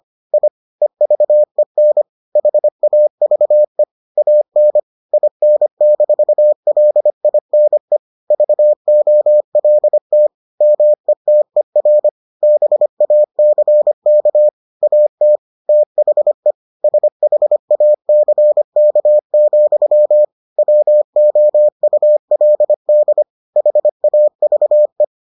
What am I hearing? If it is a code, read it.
Morse code, heard as a keyed tone, about 25 wpm, 600 Hz. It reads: I EVEN HAVE AN IN-LINE VOLT METER BACK AT THE SHACK, WOULD HAVE